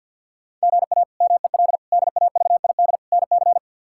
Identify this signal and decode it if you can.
Morse code, heard as a keyed tone, about 46 wpm, 700 Hz. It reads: GA MEL DAVID NC
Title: K1USN SST QSO Practice - 50wpm